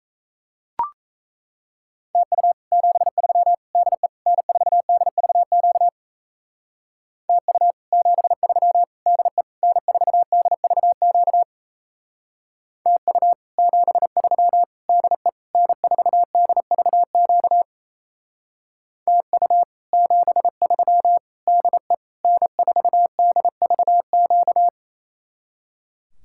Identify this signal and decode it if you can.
Morse code, heard as a keyed tone, about 38 wpm, 700 Hz. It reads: TU 73 DE N4DVQ TU 73 DE N4DVQ TU 73 DE N4DVQ TU 73 DE N4DVQ